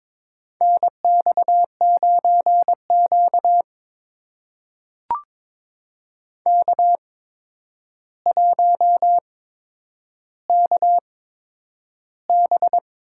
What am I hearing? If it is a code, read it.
Morse code, heard as a keyed tone, about 21 wpm, spaced as for 9 wpm, 700 Hz. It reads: NX9Q K1KB
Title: USA Call Signs - 4 Characters - 8x Character Spacing - 22wpm